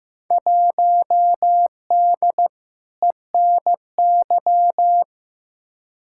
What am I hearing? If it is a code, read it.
Morse code, heard as a keyed tone, about 15 wpm, 700 Hz. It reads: 1D ENY
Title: Field Day Exchanges Encore - 15wpm